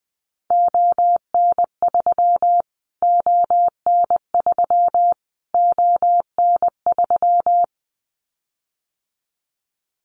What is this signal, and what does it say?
Morse code, heard as a keyed tone, about 20 wpm, 700 Hz. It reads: ON3 ON3 ON3
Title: Callsign Prefixes for Top 25 DXCC - Repeated 3x - 20wpm